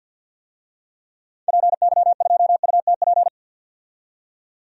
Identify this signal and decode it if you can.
Morse code, heard as a keyed tone, about 52 wpm, 700 Hz. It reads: PY2UTP
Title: DXCC-100 Call Signs - 6 Characters - 50wpm